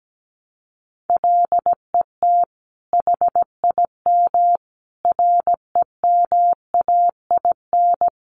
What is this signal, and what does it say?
Morse code, heard as a keyed tone, about 17 wpm, 700 Hz. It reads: LET HIM REMAIN
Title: Sentences from Top 1200 Words 17wpm